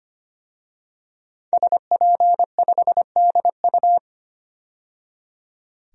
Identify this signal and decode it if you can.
Morse code, heard as a keyed tone, about 25 wpm, 700 Hz. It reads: SP5DU
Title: DXCC-100 Call Signs - 5 Characters - 25wpm